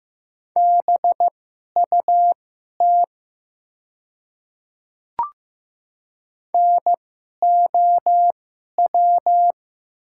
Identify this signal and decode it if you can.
Morse code, heard as a keyed tone, about 15 wpm, spaced as for 11 wpm, 700 Hz. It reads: BUT NOW
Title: 2x Character Spacing - Top 200 Words - 15wpm